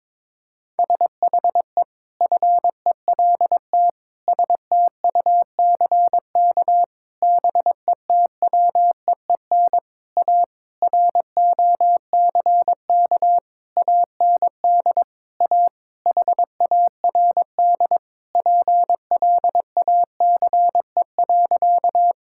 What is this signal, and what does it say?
Morse code, heard as a keyed tone, about 22 wpm, 700 Hz. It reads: SHE FELT STUCK BETWEEN A ROCK AND A HARD PLACE.
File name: practice / 22wpm / Sentences with 8-Word Phrases - 22wpm